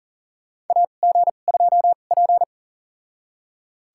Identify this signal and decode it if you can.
Morse code, heard as a keyed tone, about 40 wpm, spaced as for 29 wpm, 700 Hz. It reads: AG2P
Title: USA Call Signs - 2x Character Spacing - 40wpm